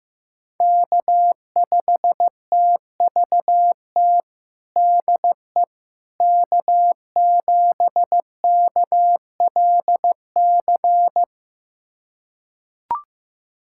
Morse code, heard as a keyed tone, about 15 wpm, 700 Hz. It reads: K5TVT DE K7KLC
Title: USA Callsign DE Callsign - 5 Characters - 15wpm